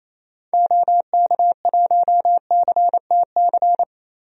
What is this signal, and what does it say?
Morse code, heard as a keyed tone, about 28 wpm, 700 Hz. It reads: OK1CTC
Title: DXCC-100 Call Signs - 6 Characters - 28wpm